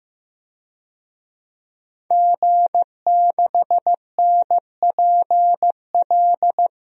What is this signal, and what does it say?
Morse code, heard as a keyed tone, about 15 wpm, 700 Hz. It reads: G6NPL
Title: DXCC-100 Call Signs - 5 Characters - 15wpm